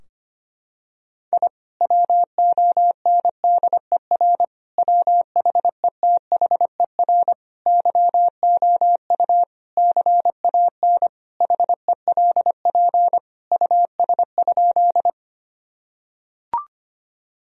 Morse code, heard as a keyed tone, about 25 wpm, 700 Hz. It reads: I WONDER WHETHER YOU CAN HELP US?
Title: Sentences from Top 1000 Words - Review Part 3 - 25wpm